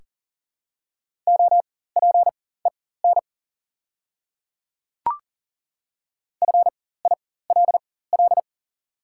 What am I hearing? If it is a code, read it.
Morse code, heard as a keyed tone, about 37 wpm, spaced as for 18 wpm, 690 Hz. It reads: OPEN FILL
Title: 4x Character Spacing - Top 500 Words - 40wpm